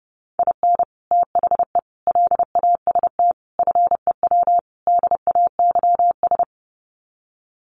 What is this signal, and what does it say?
Morse code, heard as a keyed tone, about 30 wpm, 700 Hz. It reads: IN THE LAST FEW DAYS
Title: Sets of 5 Words Encore2 30wpm